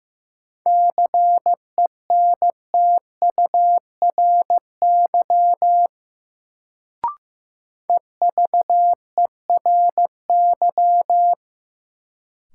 Morse code, heard as a keyed tone, about 15 wpm, 700 Hz. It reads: CENTURY EVERY
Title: Sets of 5 Words - New Word Drill - 15wpm